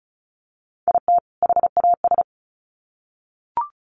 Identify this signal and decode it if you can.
Morse code, heard as a keyed tone, about 34 wpm, 700 Hz. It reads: IT HAS
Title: Sets of 2 Words 35wpm